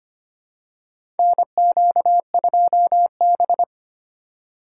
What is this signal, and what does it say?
Morse code, heard as a keyed tone, about 25 wpm, 700 Hz. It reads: NQ2B
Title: USA Call Signs - 4 Characters - 2x Character Spacing - 25wpm